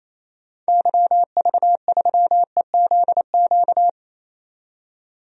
Morse code, heard as a keyed tone, about 28 wpm, 700 Hz. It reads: YV3EZQ
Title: DXCC-100 Call Signs - 6 Characters - 28wpm